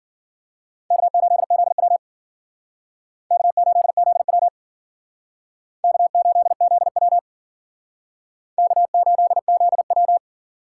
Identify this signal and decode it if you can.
Morse code, heard as a keyed tone, about 51 wpm, 700 Hz. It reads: K8ZW K8ZW K8ZW K8ZW